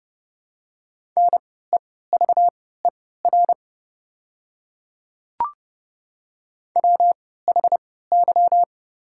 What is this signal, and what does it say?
Morse code, heard as a keyed tone, about 30 wpm, spaced as for 17 wpm, 700 Hz. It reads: NEVER WHY
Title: 3x Character Spacing - Top 300 Words - 30wpm